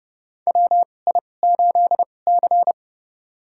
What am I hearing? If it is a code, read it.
Morse code, heard as a keyed tone, about 30 wpm, spaced as for 22 wpm, 700 Hz. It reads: WI8C
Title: USA Call Signs - 4 Characters - 2x Character Spacing - 30wpm